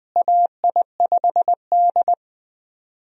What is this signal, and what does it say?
Morse code, heard as a keyed tone, about 20 wpm, 700 Hz. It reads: AI5D
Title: USA Call Signs - 4 Characters - Repeated 3x - 20wpm